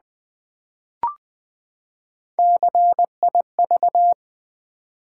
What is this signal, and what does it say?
Morse code, heard as a keyed tone, about 20 wpm, 700 Hz. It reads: CIV
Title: Build A Word - 3 Letters At A Time - 8-Letter Words - 20wpm